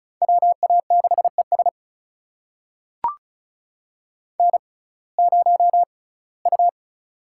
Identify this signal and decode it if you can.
Morse code, heard as a keyed tone, about 36 wpm, spaced as for 21 wpm, 700 Hz. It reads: WA6ES N0U
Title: USA Call Signs - 5 Characters - 6x Character Spacing - 35wpm